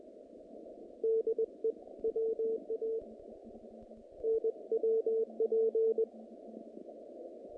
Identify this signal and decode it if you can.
Morse code, heard as a keyed tone, about 20 wpm, spaced as for 14 wpm, 450 Hz. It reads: DE WANWP
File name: contests / naqp-cw-k5tr-op-k5pi- -2300